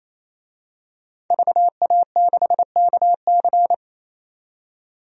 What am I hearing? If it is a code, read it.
Morse code, heard as a keyed tone, about 28 wpm, 690 Hz. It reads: VA6KC
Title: DXCC-100 Call Signs - 5 Characters - 28wpm